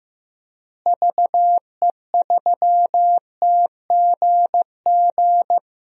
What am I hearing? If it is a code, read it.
Morse code, heard as a keyed tone, about 15 wpm, 700 Hz. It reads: VE3TGG
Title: DXCC-100 Call Signs - 6 Characters - Repeated 3x - 15wpm